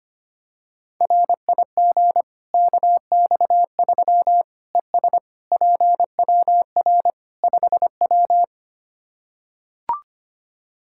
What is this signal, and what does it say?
Morse code, heard as a keyed tone, about 25 wpm, 700 Hz. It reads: RIG KX3 ES PWR 5W
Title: Practice QSO - Exchange 2 - Info - 25wpm